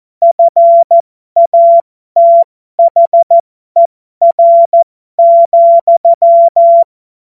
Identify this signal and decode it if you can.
Morse code, heard as a keyed tone, about 14 wpm, spaced as for 12 wpm, 680 Hz. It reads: FATHER,
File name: Morsekurs / Witze